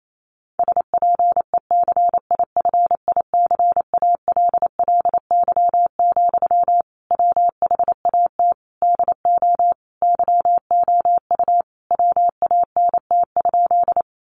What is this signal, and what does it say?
Morse code, heard as a keyed tone, about 28 wpm, 700 Hz. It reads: SPECIFICALLY, WHAT DO YOU WANT?
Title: Sentences from Top 1100 Words 28wpm